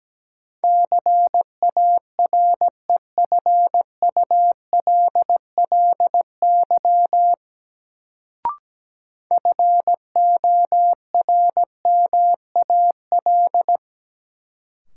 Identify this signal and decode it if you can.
Morse code, heard as a keyed tone, about 17 wpm, 700 Hz. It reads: CAREFULLY FORMAL
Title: Sentences from Top 1300 Words - New Word Drill - 17wpm